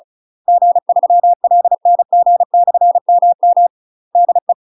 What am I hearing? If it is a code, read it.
Morse code, heard as a keyed tone, about 36 wpm, 700 Hz. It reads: G3LNG/MM DE
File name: UK MASTER QSOs TRAINING / 35WPM / UK MASTER QSOs 0018